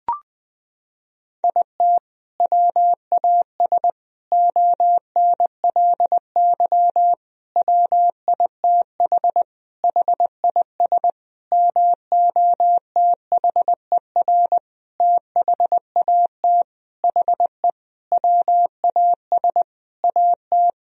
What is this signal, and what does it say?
Morse code, heard as a keyed tone, about 20 wpm, 700 Hz. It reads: IT WAS ONLY WITH HIS MOTHER THAT HE WAS AT
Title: Sentences from Top 1000 Words - Review Part 1 - 20wpm